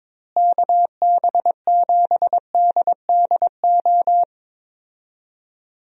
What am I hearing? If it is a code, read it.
Morse code, heard as a keyed tone, about 22 wpm, 700 Hz. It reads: KB7DDO
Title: USA Call Signs - 6 Characters - Repeated 2x - 22wpm